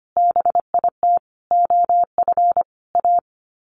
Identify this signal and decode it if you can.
Morse code, heard as a keyed tone, about 25 wpm, 700 Hz. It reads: BIT OF A
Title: Sets of 4 Words 25wpm